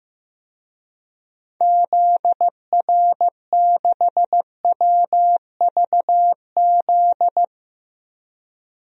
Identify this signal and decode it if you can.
Morse code, heard as a keyed tone, about 15 wpm, 700 Hz. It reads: ZR6WVZ
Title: DXCC-100 Call Signs - 6 Characters - 15wpm